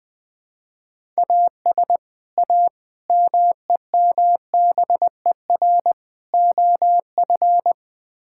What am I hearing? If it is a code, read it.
Morse code, heard as a keyed tone, about 20 wpm, 700 Hz. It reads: AS A MEMBER OF
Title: Build A Phrase - 5-Word Phrases - 20wpm